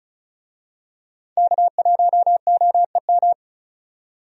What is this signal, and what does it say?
Morse code, heard as a keyed tone, about 34 wpm, 700 Hz. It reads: K1OEM
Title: USA Call Signs - 5 Characters - 35wpm